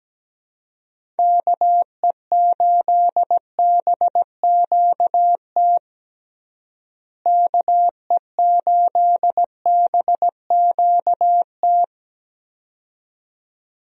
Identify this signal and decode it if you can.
Morse code, heard as a keyed tone, about 17 wpm, 700 Hz. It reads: KE8BQT KE8BQT
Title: USA Call Signs - 6 Characters - Repeated 2x - 17wpm